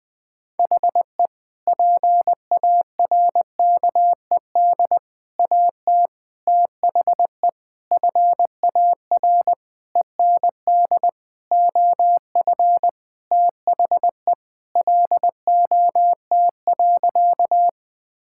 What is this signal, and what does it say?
Morse code, heard as a keyed tone, about 20 wpm, 700 Hz. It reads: HE PARKED AT THE FAR END OF THE LOT.
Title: Sentences with 5-Word Phrases - 20wpm